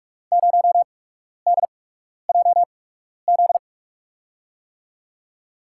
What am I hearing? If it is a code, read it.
Morse code, heard as a keyed tone, about 44 wpm, spaced as for 12 wpm, 700 Hz. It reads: 0DJZ
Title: USA Call Signs - 6 Characters - 8x Character Spacing - 45wpm